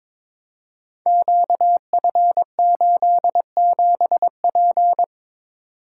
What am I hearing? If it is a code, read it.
Morse code, heard as a keyed tone, about 22 wpm, 700 Hz. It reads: QF87P
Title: Code Groups - 5 Letter:Numbers - 22wpm